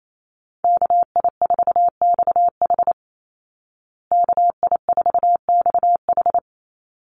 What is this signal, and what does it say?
Morse code, heard as a keyed tone, about 28 wpm, 700 Hz. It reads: KI4XH KI4XH
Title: Call Signs Repeated 3x - 28wpm